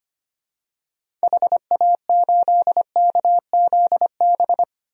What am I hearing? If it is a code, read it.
Morse code, heard as a keyed tone, about 25 wpm, 700 Hz. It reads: HA8KZB
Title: DXCC-100 Call Signs - 6 Characters - 25wpm